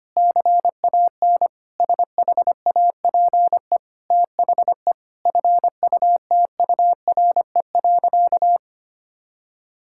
Morse code, heard as a keyed tone, about 25 wpm, 700 Hz. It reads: CAN SHAPE THE FUTURE.
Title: Sentences from Top 1000 Words - Review Part 6 - 25wpm